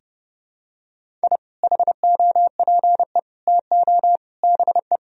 Morse code, heard as a keyed tone, about 32 wpm, 700 Hz. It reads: I HOPE TO BE
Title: Sentences from Top 1000 Words - Review Part 1 - 30wpm